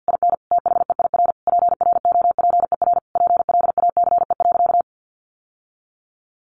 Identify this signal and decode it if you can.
Morse code, heard as a keyed tone, about 50 wpm, 700 Hz. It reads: IN THEIR PROPER PLACE.
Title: Sentences from Top 800 Words 50wpm